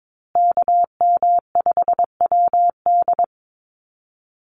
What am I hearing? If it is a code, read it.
Morse code, heard as a keyed tone, about 22 wpm, 700 Hz. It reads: KM5WD
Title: USA Call Signs Repeated 2x 22wpm